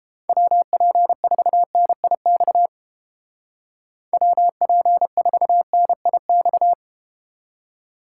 Morse code, heard as a keyed tone, about 30 wpm, 700 Hz. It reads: WP4NIX WP4NIX